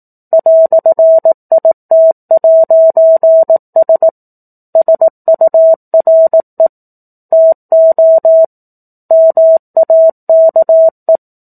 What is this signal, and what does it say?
Morse code, heard as a keyed tone, about 18 wpm, 650 Hz. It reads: "IT'S SURE TO MAKE
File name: 04-Chapter-04 18 WPM